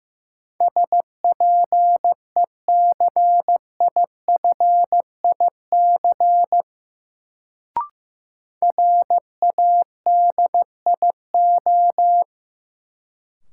Morse code, heard as a keyed tone, about 15 wpm, 700 Hz. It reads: SPECIFIC RADIO
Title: Sentences from Top 1000 Words - Review Part 3 - New Word Drill - 15wpm